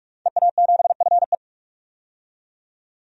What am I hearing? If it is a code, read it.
Morse code, heard as a keyed tone, about 43 wpm, 700 Hz. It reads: EA7FE
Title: DXCC-100 Call Signs - 5 Characters - 45wpm